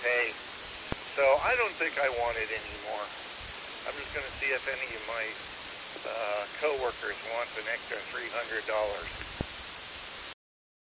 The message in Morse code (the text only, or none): none